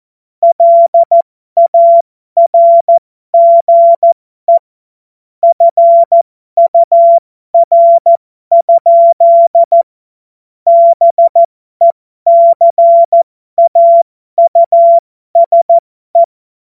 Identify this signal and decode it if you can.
Morse code, heard as a keyed tone, about 14 wpm, spaced as for 12 wpm, 680 Hz. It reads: LARGE FUR? BECAUSE
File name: Morsekurs / Witze